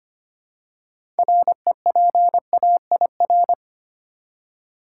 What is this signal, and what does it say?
Morse code, heard as a keyed tone, about 25 wpm, 700 Hz. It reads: REPAIR
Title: Six Letter Words - 25wpm